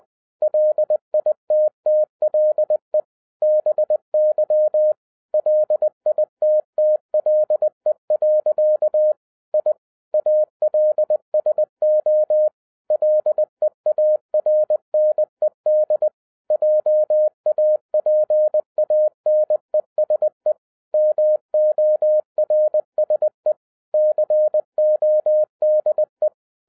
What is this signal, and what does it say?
Morse code, heard as a keyed tone, about 20 wpm, 610 Hz. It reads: LITTLE BY LITTLE. I ALSO LEARNED JAPANESE MORSE CODE
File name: 2012-01-mp3 / n05-CAME BACK TO HAM RADIO AFTER 30 YEARS OF INTERVAL-20wpm